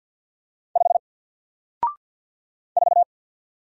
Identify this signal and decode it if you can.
Morse code, heard as a keyed tone, about 50 wpm, 700 Hz. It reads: 5 4
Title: Single Character (Letter:Number:Punctuation:Prosign) 50wpm